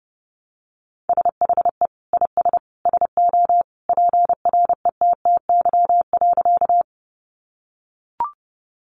Morse code, heard as a keyed tone, about 30 wpm, 700 Hz. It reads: SHE IS SO PRETTY.
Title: Sentences from Top 1000 Words 30wpm